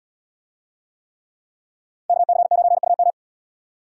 Morse code, heard as a keyed tone, about 72 wpm, 700 Hz. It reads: KC9UM